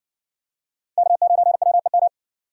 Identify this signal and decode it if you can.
Morse code, heard as a keyed tone, about 60 wpm, 700 Hz. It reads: K9WEM